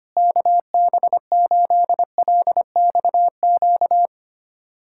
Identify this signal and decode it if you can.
Morse code, heard as a keyed tone, about 25 wpm, 700 Hz. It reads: KB8LXQ
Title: USA Callsign DE Callsign - 6 Characters - 25wpm